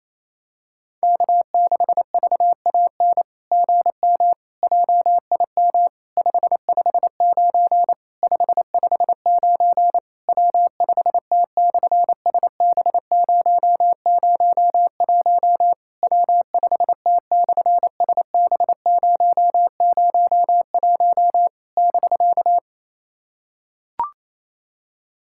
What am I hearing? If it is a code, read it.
Morse code, heard as a keyed tone, about 28 wpm, 700 Hz. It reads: K6VAN GM JIM 559 559 W5T/SB001 W5T/SB001 <BK>
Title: SOTA QSO Practice 28wpm